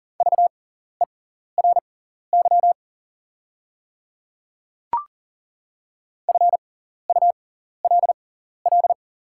Morse code, heard as a keyed tone, about 40 wpm, spaced as for 14 wpm, 690 Hz. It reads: VERY FULL
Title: 6x Character Spacing - Top 500 Words - 40wpm